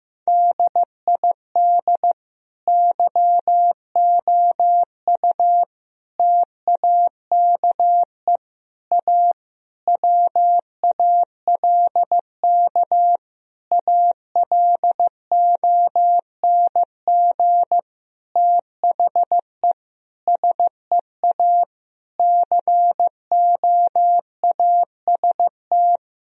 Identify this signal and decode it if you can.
Morse code, heard as a keyed tone, about 15 wpm, 700 Hz. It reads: DID YOU TAKE A WALK ALONG THE SEA COAST
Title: Sentences from Top 1000 Words - Review Part 4 - 15wpm